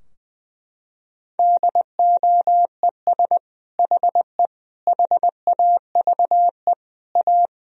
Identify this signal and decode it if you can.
Morse code, heard as a keyed tone, about 21 wpm, 700 Hz. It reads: DOES HE HAVE A
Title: Sentences from Top 1000 Words - Review Part 2 - 20wpm